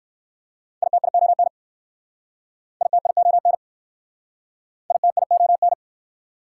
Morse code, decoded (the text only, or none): ITION ITION ITION